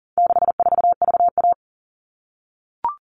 Morse code, heard as a keyed tone, about 40 wpm, 700 Hz. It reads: 64VA
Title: All Cut Numbers - 4-Digit 40wpm